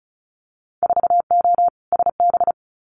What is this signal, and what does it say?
Morse code, heard as a keyed tone, about 34 wpm, 700 Hz. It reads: 4O SB
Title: Winter Field Day Exchanges 35wpm